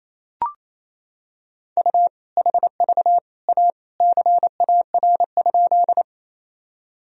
Morse code, heard as a keyed tone, about 28 wpm, 700 Hz. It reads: U HV A CAR?
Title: Sentences from Top 300 Words and Abbreviations 28wpm